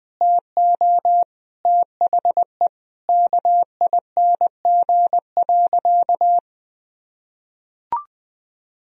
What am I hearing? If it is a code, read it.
Morse code, heard as a keyed tone, about 19 wpm, 700 Hz. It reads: TO THE KING.
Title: Sentences from Top 1000 Words - Review Part 3 - 20wpm